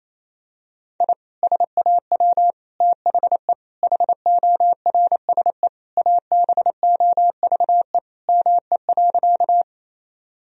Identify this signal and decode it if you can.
Morse code, heard as a keyed tone, about 28 wpm, 700 Hz. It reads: I SAW THE HORSE ABOVE ME.
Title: Sentences from Top 1000 Words - Review Part 2 - 28wpm